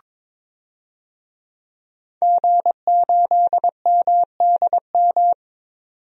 Morse code, heard as a keyed tone, about 22 wpm, 700 Hz. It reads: G8MDM
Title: DXCC-100 Call Signs - 5 Characters - 22wpm